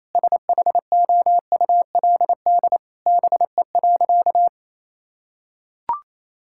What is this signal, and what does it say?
Morse code, heard as a keyed tone, about 28 wpm, 700 Hz. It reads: SHOULD BE.
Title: Sentences from Top 1000 Words - Review Part 6 - 28wpm